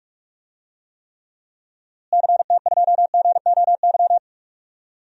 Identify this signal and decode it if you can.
Morse code, heard as a keyed tone, about 45 wpm, 700 Hz. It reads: CT2GOY
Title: DXCC-100 Call Signs - 6 Characters - Repeated 2x - 45wpm